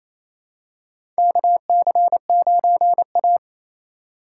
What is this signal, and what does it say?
Morse code, heard as a keyed tone, about 28 wpm, 700 Hz. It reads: KC9A
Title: USA Call Signs - 4 Characters - 2x Character Spacing - 28wpm